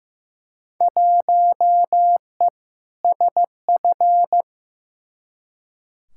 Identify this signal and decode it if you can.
Morse code, heard as a keyed tone, about 15 wpm, 700 Hz. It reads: 1E SF
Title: Field Day Exchanges - 15wpm